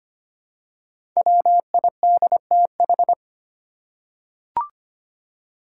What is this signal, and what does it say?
Morse code, heard as a keyed tone, about 25 wpm, 700 Hz. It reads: WIDTH
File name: 5 Letter Words - First 3 Letters Given - 25wpm